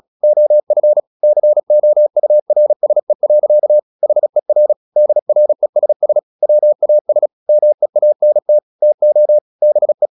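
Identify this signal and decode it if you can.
Morse code, heard as a keyed tone, about 36 wpm, 600 Hz. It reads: OF COURSE. HER DRESS WAS MEANT TO BE
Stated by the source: Morse practice at 36 wpm, text of Ken Follett's novel Never, from CW med Gnister.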